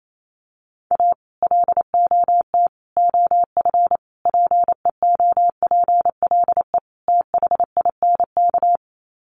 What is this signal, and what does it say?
Morse code, heard as a keyed tone, about 28 wpm, 700 Hz. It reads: A LOT OF PEOPLE THINK
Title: Sets of 5 Words Encore 28wpm